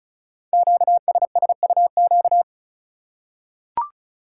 Morse code, heard as a keyed tone, about 36 wpm, 690 Hz. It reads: QSSUQ
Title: Code Groups - 5 Letters - 35wpm